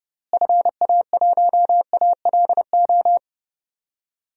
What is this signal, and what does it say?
Morse code, heard as a keyed tone, about 30 wpm, 700 Hz. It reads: FA1ALO
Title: DXCC-100 Call Signs - 6 Characters - Repeated 2x - 30wpm